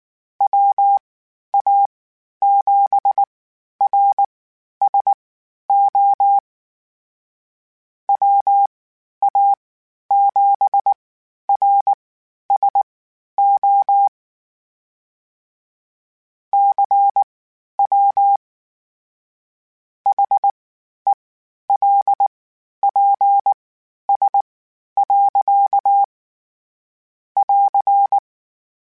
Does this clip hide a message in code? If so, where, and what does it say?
Yes: Morse code from 0.4 to 28.2 s, heard as a keyed tone, about 19 wpm, spaced as for 11 wpm, 800 Hz. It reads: WA7RSO WA7RSO CW HELPS. <AR>